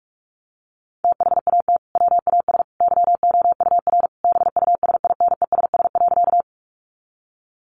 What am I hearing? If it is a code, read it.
Morse code, heard as a keyed tone, about 44 wpm, 700 Hz. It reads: THAT WAS YOUR BUSINESS.
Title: Sentences from Top 600 Words 45wpm